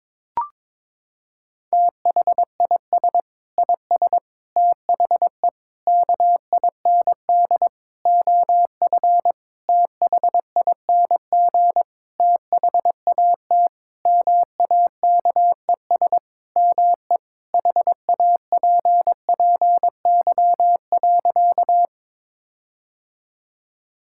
Morse code, heard as a keyed tone, about 22 wpm, 700 Hz. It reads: THIS IS THE KIND OF THING THAT MAKES ME HAPPY.
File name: Sentences with 7-Word Phrases - 22wpm